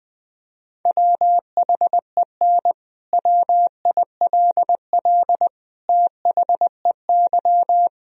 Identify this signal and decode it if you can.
Morse code, heard as a keyed tone, about 20 wpm, 700 Hz. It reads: WHEN WILL THEY
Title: Sentences from Top 1000 Words - Review Part 4 - 20wpm